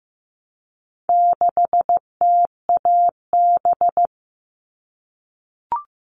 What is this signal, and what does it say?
Morse code, heard as a keyed tone, about 15 wpm, 700 Hz. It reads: 6TAB
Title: All Cut Numbers - 4-Digit 15wpm